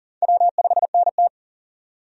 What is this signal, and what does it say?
Morse code, heard as a keyed tone, about 40 wpm, 700 Hz. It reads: W5NT
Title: USA Call Signs - 4 Characters - 40wpm